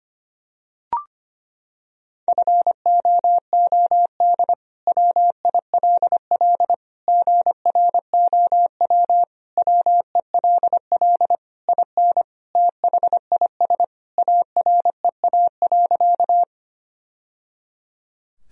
Morse code, heard as a keyed tone, about 25 wpm, 700 Hz. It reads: FOOD WILL GROW WELL IN THIS AREA.
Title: Sentences from Top 1000 Words - Review Part 2 - 25wpm